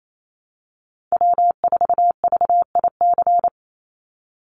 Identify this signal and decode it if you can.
Morse code, heard as a keyed tone, about 28 wpm, 700 Hz. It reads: W4VIC
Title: Call Signs Repeated 2x - 28wpm